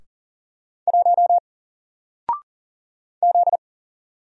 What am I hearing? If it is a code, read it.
Morse code, heard as a keyed tone, about 41 wpm, 700 Hz. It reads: 1 Z
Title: Single Character (Letter:Number:Punctuation:Prosign) Alt 40wpm